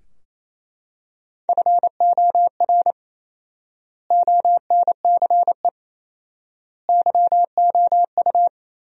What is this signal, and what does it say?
Morse code, heard as a keyed tone, about 28 wpm, 700 Hz. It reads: FOR ONCE YOU
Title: 4x Word-Spacing - Sentences from Top 300 Words - 28wpm